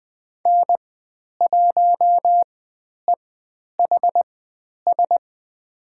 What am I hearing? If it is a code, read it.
Morse code, heard as a keyed tone, about 20 wpm, spaced as for 10 wpm, 700 Hz. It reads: N1EHS
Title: 109 Lesson 18g - Letter m - Call Signs 20wpm